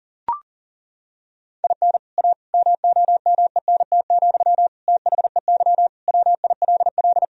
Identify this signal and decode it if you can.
Morse code, heard as a keyed tone, about 40 wpm, 700 Hz. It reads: IN A MOMENT, THEY WILL
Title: Sentences from Top 1000 Words - Review Part 2 - 40wpm